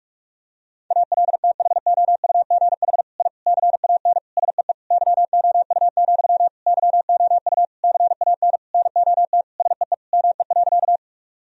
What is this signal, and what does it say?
Morse code, heard as a keyed tone, about 46 wpm, 700 Hz. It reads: ALTHOUGH I CAN SEE YOU, YOU CAN NOT SEE ME.
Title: Sentences from Top 1000 Words - Review Part 3 - 45wpm